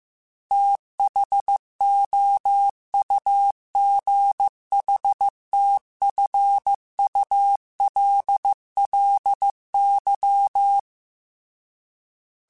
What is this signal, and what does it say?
Morse code, heard as a keyed tone, about 15 wpm, 800 Hz. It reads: THOUGHTFULLY
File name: TH Runs / TH Runs mp3s / TH Runs @13